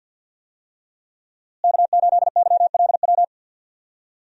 Morse code, heard as a keyed tone, about 52 wpm, 700 Hz. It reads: K8YLW
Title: USA Call Signs - 5 Characters - Repeated 3x - 50wpm